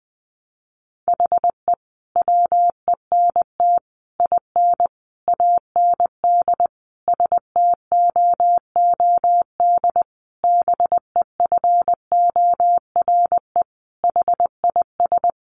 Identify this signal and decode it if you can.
Morse code, heard as a keyed tone, about 20 wpm, 700 Hz. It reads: HE WENT IN AND STOOD BEFORE HIS
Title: Sentences from Top 1300 Words 20wpm